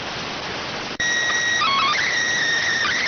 SSTV (slow-scan television), PD 290.